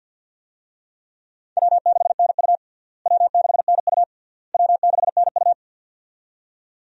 Morse code, heard as a keyed tone, about 46 wpm, 700 Hz. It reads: W6NU W6NU W6NU
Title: USA Call Signs - 4 Characters - Repeated 3x - 50wpm